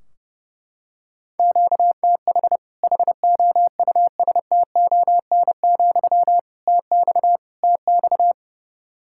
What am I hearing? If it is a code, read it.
Morse code, heard as a keyed tone, about 32 wpm, 700 Hz. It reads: QTH HOUSTON, TX TX
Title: Practice QSO - Exchange 1 - Beginning - 30wpm